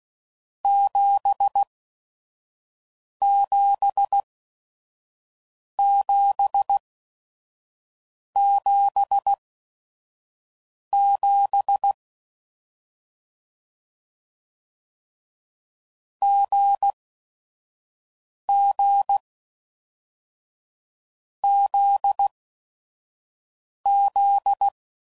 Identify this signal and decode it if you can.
Morse code, heard as a keyed tone, about 16 wpm, spaced as for 5 wpm, 800 Hz. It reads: 77777 GG ZZ
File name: Combinations / Difficult Combos / GZ GZ7